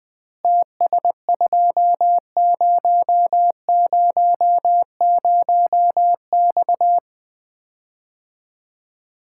Morse code, heard as a keyed tone, about 21 wpm, 700 Hz. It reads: TS2000X
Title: Common QSO Phrases - 3x Word-Spacing - 20wpm